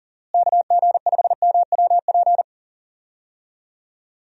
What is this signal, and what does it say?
Morse code, heard as a keyed tone, about 37 wpm, 700 Hz. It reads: KG5MWP
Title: USA Call Signs - 6 Characters - Repeated 3x - 40wpm